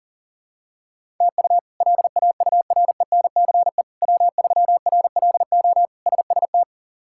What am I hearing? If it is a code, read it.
Morse code, heard as a keyed tone, about 40 wpm, 700 Hz. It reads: TU LAURENCE W3RLO SST
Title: K1USN SST QSO Practice - 40wpm